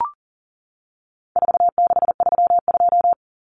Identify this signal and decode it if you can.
Morse code, heard as a keyed tone, about 40 wpm, 700 Hz. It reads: 4632
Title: Common Cut Numbers - 4-Digit 40wpm